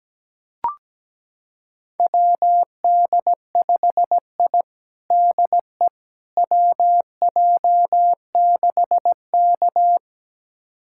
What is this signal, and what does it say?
Morse code, heard as a keyed tone, about 17 wpm, 700 Hz. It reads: WD5I DE WJ6K
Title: USA Callsign DE Callsign - 4 Characters - 17wpm